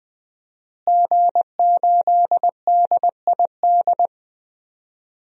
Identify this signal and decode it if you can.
Morse code, heard as a keyed tone, about 20 wpm, 700 Hz. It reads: G8DID
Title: DXCC-100 Call Signs - 5 Characters - Repeated 3x - 20wpm